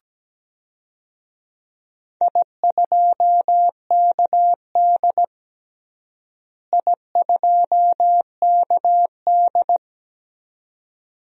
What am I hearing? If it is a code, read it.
Morse code, heard as a keyed tone, about 17 wpm, 700 Hz. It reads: I2KD I2KD
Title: DXCC-100 Call Signs - 4 Characters - Repeated 2x - 17wpm